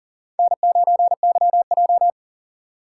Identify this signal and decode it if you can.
Morse code, heard as a keyed tone, about 40 wpm, 690 Hz. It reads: N9YJ
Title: USA Call Signs - 4 Characters - 2x Character Spacing - 40wpm